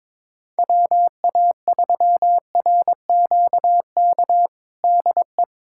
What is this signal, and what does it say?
Morse code, heard as a keyed tone, about 22 wpm, 700 Hz. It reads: WA3RQK DE
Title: USA Callsign DE Callsign - 6 Characters - 22wpm